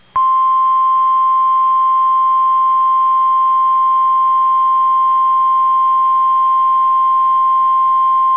Emergency Alert System (SAME) tones (the 1050 Hz attention tone). It is heard from 0.5 to 7.8 s.